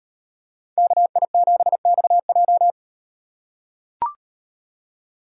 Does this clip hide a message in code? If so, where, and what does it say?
Yes: Morse code from 0.8 to 2.7 s, heard as a keyed tone, about 36 wpm, 700 Hz. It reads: KI7XJ